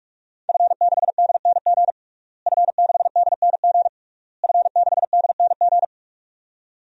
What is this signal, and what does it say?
Morse code, heard as a keyed tone, about 47 wpm, 700 Hz. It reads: F6DNG F6DNG F6DNG
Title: DXCC-100 Call Signs - 5 Characters - Repeated 3x - 45wpm